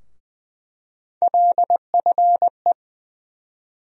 Morse code, heard as a keyed tone, about 20 wpm, 700 Hz. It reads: LFE
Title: USA Call Sign Postfixes - 3 Letters 20wpm